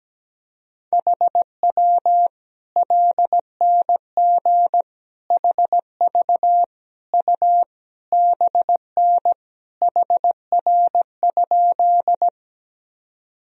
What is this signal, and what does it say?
Morse code, heard as a keyed tone, about 17 wpm, 700 Hz. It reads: HW LNG HV U BN HR?
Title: Sentences from Top 200 Words and Abbreviations 17wpm